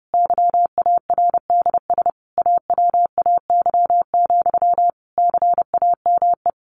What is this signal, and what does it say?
Morse code, heard as a keyed tone, about 30 wpm, 700 Hz. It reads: YARDS AWAY, CAME
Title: A Sound Of Thunder 30wpm